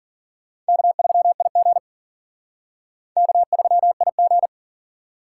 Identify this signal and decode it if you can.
Morse code, heard as a keyed tone, about 51 wpm, 700 Hz. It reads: K3IG K3IG